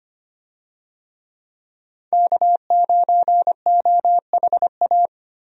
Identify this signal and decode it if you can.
Morse code, heard as a keyed tone, about 25 wpm, 700 Hz. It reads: K9OHA
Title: USA Call Signs - 5 Characters - Repeated 3x - 25wpm